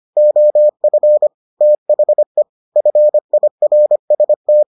Morse code, heard as a keyed tone, about 25 wpm, 600 Hz. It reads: OF THE FIRST